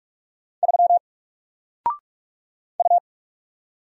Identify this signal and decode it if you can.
Morse code, heard as a keyed tone, about 44 wpm, 700 Hz. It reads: 3 U
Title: Single Letter-Number 45wpm